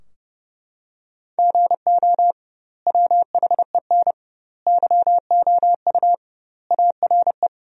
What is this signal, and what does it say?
Morse code, heard as a keyed tone, about 30 wpm, 700 Hz. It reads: GO WHEN YOU ARE
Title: 2x Word-Spacing - Sentences from Top 500 Words - 30wpm